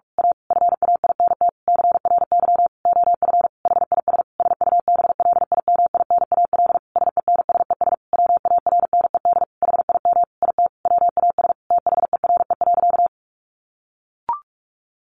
Morse code, heard as a keyed tone, about 44 wpm, 700 Hz. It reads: A FAINT CRY OF HIS SUBLIMINAL SENSES WARNED HIM IT WAS THERE.
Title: A Sound Of Thunder 45wpm